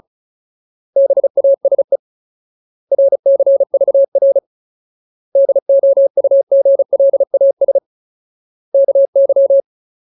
Morse code, heard as a keyed tone, about 34 wpm, 550 Hz. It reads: BASE RCVR DOUGLAS KY
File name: Ultimate Word List / 35WPM Ultimate